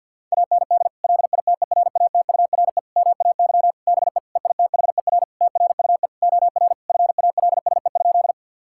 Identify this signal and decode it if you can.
Morse code, heard as a keyed tone, about 50 wpm, 700 Hz. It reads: AND LITERATURE MAY BE EITHER TRUE OR FALSE?
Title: Sentences from Top 1600 Words 50wpm